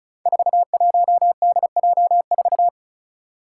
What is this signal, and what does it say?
Morse code, heard as a keyed tone, about 36 wpm, 700 Hz. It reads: 41DJ4
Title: Five Challenging Letter-Number Combinations 35wpm